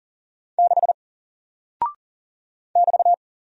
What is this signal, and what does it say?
Morse code, heard as a keyed tone, about 40 wpm, 700 Hz. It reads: 6 <BT>
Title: Single Character (Letter:Number:Punctuation:Prosign) Alt 40wpm